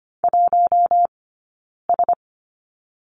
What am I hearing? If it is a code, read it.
Morse code, heard as a keyed tone, about 25 wpm, 700 Hz. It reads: 1 S
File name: Lesson 15d - Number 5 - Two Characters 25wpm